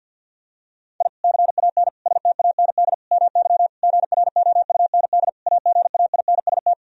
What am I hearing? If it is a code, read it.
Morse code, heard as a keyed tone, about 46 wpm, 700 Hz. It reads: I CAN STAND MY GROUND AGAINST
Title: Sentences from Top 1000 Words - Review Part 2 - 50wpm